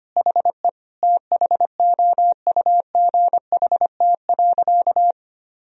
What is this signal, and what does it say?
Morse code, heard as a keyed tone, about 25 wpm, 700 Hz. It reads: HE THOUGHT.
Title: Sentences from Top 1000 Words - Review Part 2 - 25wpm